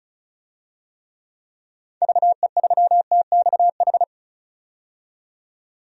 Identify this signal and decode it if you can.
Morse code, heard as a keyed tone, about 36 wpm, 690 Hz. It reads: VE3TXH
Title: DXCC-100 Call Signs - 6 Characters - 35wpm